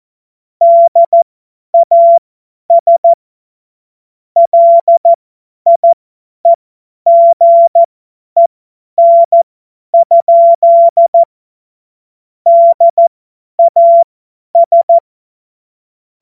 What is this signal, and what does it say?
Morse code, heard as a keyed tone, about 14 wpm, spaced as for 10 wpm, 680 Hz. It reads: DAS LIEGEN? DAS
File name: Morsekurs / Witze / Witz 24 Juni 2025 50 F70